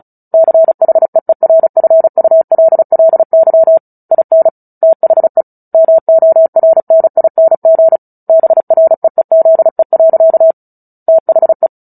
Morse code, heard as a keyed tone, about 35 wpm, 650 Hz. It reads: CHEERFULLY IN THE MORNING BREEZE. THE
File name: Book Two / 09-Chapter 9 - 35 WPM